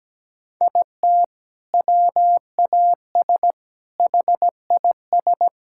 Morse code, heard as a keyed tone, about 17 wpm, 700 Hz. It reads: IT WAS HIS
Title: Sentences from Top 1600 Words 17wpm